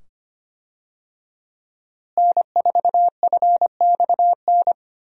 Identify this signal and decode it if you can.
Morse code, heard as a keyed tone, about 25 wpm, 700 Hz. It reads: N4FXN
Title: USA Call Signs - 5 Characters - Repeated 2x - 25wpm